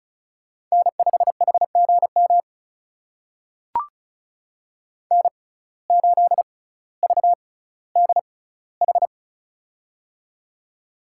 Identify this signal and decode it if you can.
Morse code, heard as a keyed tone, about 36 wpm, spaced as for 18 wpm, 700 Hz. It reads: N5HGM N8VDH
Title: USA Call Signs - 5 Characters - 6x Character Spacing - 35wpm